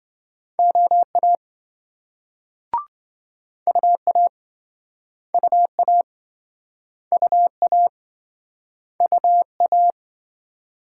Morse code, heard as a keyed tone, about 26 wpm, 700 Hz. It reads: OA UA UA UA UA